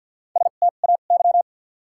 Morse code, heard as a keyed tone, about 53 wpm, spaced as for 37 wpm, 700 Hz. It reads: STAY
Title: 2x Character Spacing - Top 500 Words - 50wpm